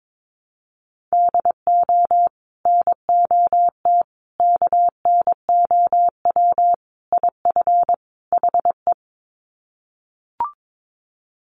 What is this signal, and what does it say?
Morse code, heard as a keyed tone, about 22 wpm, 700 Hz. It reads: DO NOT KNOW IF HE
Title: Sets of 5 Words Encore2 22wpm